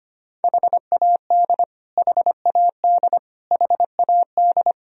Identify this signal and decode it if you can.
Morse code, heard as a keyed tone, about 25 wpm, 700 Hz. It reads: HAD HAD HAD
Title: Top 100 Words Repeated 25wpm